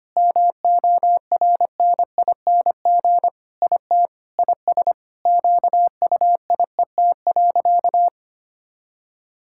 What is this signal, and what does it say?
Morse code, heard as a keyed tone, about 25 wpm, 700 Hz. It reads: MORNING IT IS QUIET.
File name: practice / 25wpm / Sentences with 3-Word Phrases - 25wpm